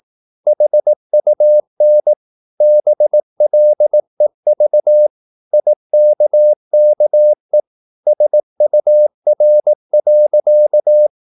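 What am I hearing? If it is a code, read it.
Morse code, heard as a keyed tone, about 18 wpm, 600 Hz. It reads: HUN BLEV IKKE SUR.
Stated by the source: Slow (QRS) Morse practice at 18 wpm, CW med Gnister.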